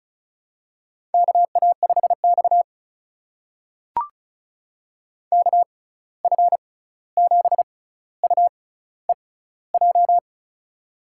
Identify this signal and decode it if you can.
Morse code, heard as a keyed tone, about 35 wpm, spaced as for 16 wpm, 700 Hz. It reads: KA5X KF7UEJ
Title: USA Call Signs - 6x Character Spacing - 35wpm